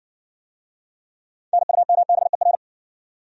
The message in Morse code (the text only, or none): NUMBER